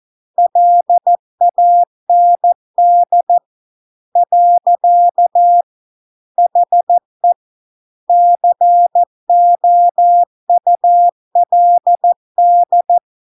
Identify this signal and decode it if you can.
Morse code, heard as a keyed tone, about 14 wpm, 700 Hz. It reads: LAND . HE COULD